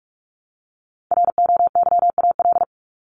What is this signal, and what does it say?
Morse code, heard as a keyed tone, about 44 wpm, 700 Hz. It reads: ROYAL
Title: Top 1751-2000 Words 45wpm